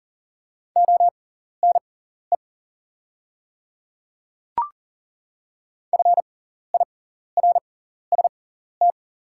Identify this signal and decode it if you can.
Morse code, heard as a keyed tone, about 40 wpm, spaced as for 14 wpm, 690 Hz. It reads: ONE FIRST
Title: 6x Character Spacing - Top 100 Words - 40wpm